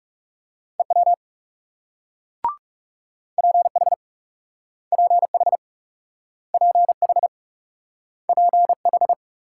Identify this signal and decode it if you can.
Morse code, heard as a keyed tone, about 40 wpm, 700 Hz. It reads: EW PH PH PH PH